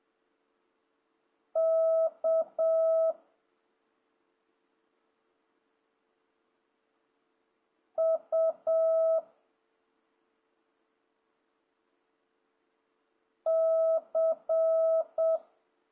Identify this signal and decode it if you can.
Morse code, heard as a keyed tone, about 7 wpm, 650 Hz. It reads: K U C
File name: jh8cbh / 2026cw